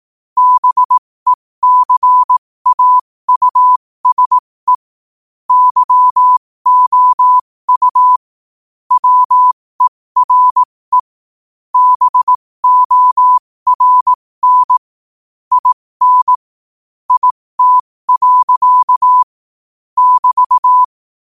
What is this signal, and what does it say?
Morse code, heard as a keyed tone, about 18 wpm, spaced as for 15 wpm, 1000 Hz. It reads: BECAUSE YOU WERE BORN IN IT. <BT>
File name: qotdfiles / 15wpm